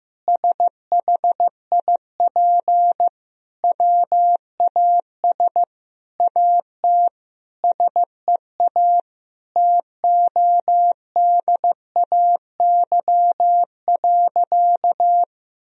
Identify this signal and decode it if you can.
Morse code, heard as a keyed tone, about 15 wpm, 700 Hz. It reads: SHIP WAS AT SEA TODAY.
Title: Sentences from Top 1000 Words - Review Part 2 - 15wpm